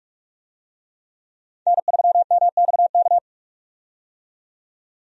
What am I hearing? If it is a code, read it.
Morse code, heard as a keyed tone, about 46 wpm, 700 Hz. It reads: N3MXK
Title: USA Call Signs - 5 Characters - 45wpm